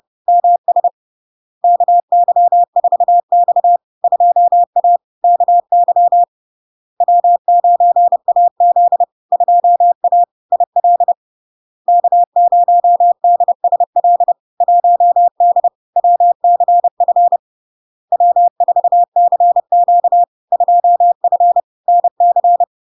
Morse code, heard as a keyed tone, about 30 wpm, 700 Hz. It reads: MS KY4X 2A KY W9AZ 2A IL K0DSL 1D WCF W4CQ 2F NC
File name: LARK-FD-2022-Log-in-CW-30wpm-700-Hz